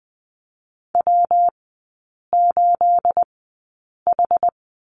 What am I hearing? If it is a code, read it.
Morse code, heard as a keyed tone, about 20 wpm, spaced as for 8 wpm, 690 Hz. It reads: W8H
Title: 212 Lesson 34e - Number 8 - Three Characters 20wpm